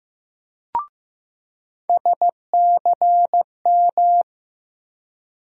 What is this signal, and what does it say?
Morse code, heard as a keyed tone, about 15 wpm, 700 Hz. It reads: SCM
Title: USA Call Sign Postfixes - 3 Letters 15wpm